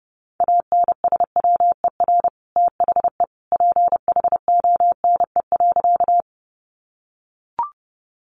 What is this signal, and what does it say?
Morse code, heard as a keyed tone, about 30 wpm, 700 Hz. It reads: ANSWER THE PHONE.
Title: Sentences from Top 1100 Words 30wpm